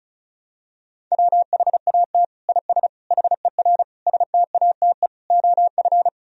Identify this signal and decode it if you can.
Morse code, heard as a keyed tone, about 34 wpm, 700 Hz. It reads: WHAT IS HER STATE OF
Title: Sentences from Top 1000 Words - Review Part 2 - 35wpm